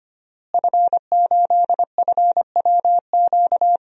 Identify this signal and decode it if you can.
Morse code, heard as a keyed tone, about 25 wpm, 700 Hz. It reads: F8FWQ
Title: DXCC-100 Call Signs - 5 Characters - Repeated 3x - 25wpm